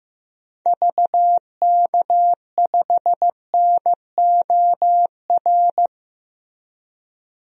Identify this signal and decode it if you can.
Morse code, heard as a keyed tone, about 15 wpm, 700 Hz. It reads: VK5NOR
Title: DXCC-100 Call Signs - 6 Characters - Repeated 3x - 15wpm